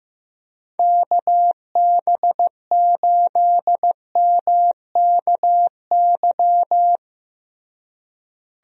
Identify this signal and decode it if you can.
Morse code, heard as a keyed tone, about 15 wpm, 700 Hz. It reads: KB8MKY
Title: USA Call Signs - 6 Characters - Repeated 3x - 15wpm